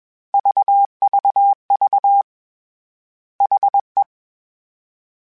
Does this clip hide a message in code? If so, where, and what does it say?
Yes: Morse code from 0.3 to 4.0 s, heard as a keyed tone, about 21 wpm, 800 Hz. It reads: VVV HE